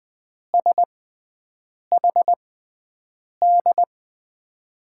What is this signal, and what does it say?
Morse code, heard as a keyed tone, about 20 wpm, spaced as for 7 wpm, 700 Hz. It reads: SHD
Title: USA Call Signs - 6 Characters - 6x Character Spacing - 20wpm